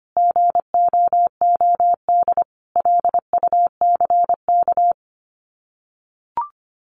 Morse code, heard as a keyed tone, about 25 wpm, 700 Hz. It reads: GOOD LUCK
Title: Common QSO Phrases 25wpm